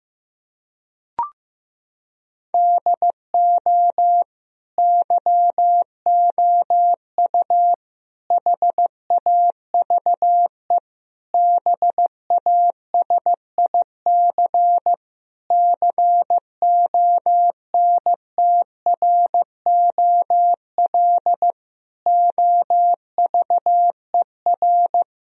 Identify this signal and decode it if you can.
Morse code, heard as a keyed tone, about 15 wpm, 700 Hz. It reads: DO YOU HAVE BASIC CONTROL OVER
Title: Sentences from Top 1000 Words - Review Part 3 - 15wpm